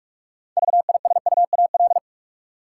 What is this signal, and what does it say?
Morse code, heard as a keyed tone, about 46 wpm, 700 Hz. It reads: VISUAL
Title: Six Letter Words - 45wpm